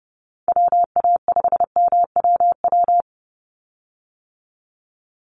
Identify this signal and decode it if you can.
Morse code, heard as a keyed tone, about 30 wpm, 700 Hz. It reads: WA5MWW
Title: USA Call Signs 30wpm